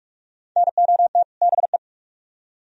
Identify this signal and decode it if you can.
Morse code, heard as a keyed tone, about 44 wpm, 700 Hz. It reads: NOT BE